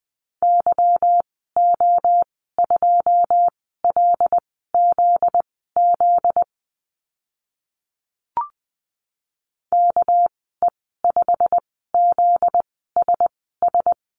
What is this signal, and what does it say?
Morse code, heard as a keyed tone, about 20 wpm, spaced as for 15 wpm, 700 Hz. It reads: YO2LZZ KE5ZSS
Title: 245 Lesson 39g - Letter z - Call Signs 20wpm